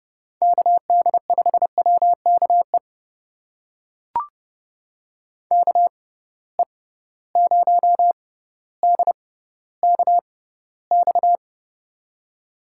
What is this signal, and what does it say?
Morse code, heard as a keyed tone, about 29 wpm, spaced as for 15 wpm, 700 Hz. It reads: KD5WKE KE0DKX
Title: USA Call Signs - 6 Characters - 6x Character Spacing - 30wpm